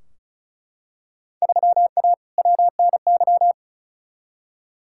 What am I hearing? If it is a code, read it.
Morse code, heard as a keyed tone, about 36 wpm, 700 Hz. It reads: 3A WNY
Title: Field Day Exchanges - 35wpm